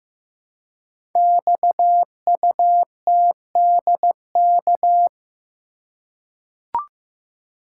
Morse code, heard as a keyed tone, about 15 wpm, 700 Hz. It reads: XUTDK
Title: Code Groups - 5 Letters - 15wpm